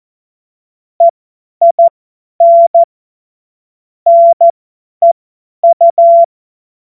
Morse code, heard as a keyed tone, about 14 wpm, spaced as for 10 wpm, 680 Hz. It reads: EIN NEU
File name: Morsekurs / Witze / Witz 30. Dezember 2025 50 F70